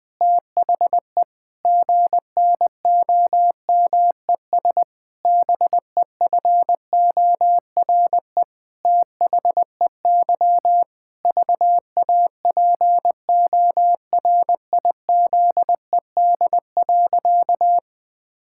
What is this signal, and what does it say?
Morse code, heard as a keyed tone, about 21 wpm, 700 Hz. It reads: THE GNOMES BEFORE THEY VAPORIZED.
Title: Pangram Challenge v2 - 20wpm